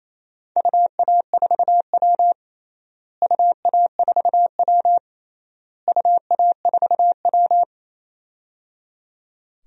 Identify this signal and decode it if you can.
Morse code, heard as a keyed tone, about 28 wpm, 700 Hz. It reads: UA4W UA4W UA4W
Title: DXCC-100 Call Signs - 4 Characters - Repeated 3x - 28wpm